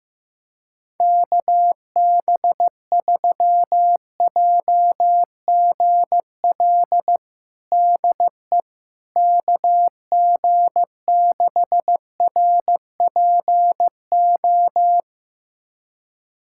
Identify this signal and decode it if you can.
Morse code, heard as a keyed tone, about 15 wpm, 700 Hz. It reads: KB3JGL DE KG6RPO
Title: USA Callsign DE Callsign 15wpm